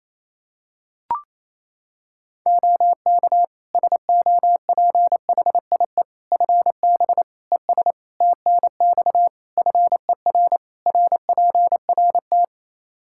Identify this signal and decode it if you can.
Morse code, heard as a keyed tone, about 28 wpm, 700 Hz. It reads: OK SOPHIE FB ES TNX FER RPRT
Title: Practice QSO - Exchange 2 - Info - 28wpm